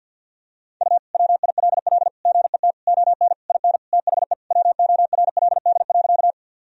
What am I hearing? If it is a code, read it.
Morse code, heard as a keyed tone, about 50 wpm, 700 Hz. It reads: U WILL GET ON IN THE WORLD.
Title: Sentences from Top 200 Words and Abbreviations 50wpm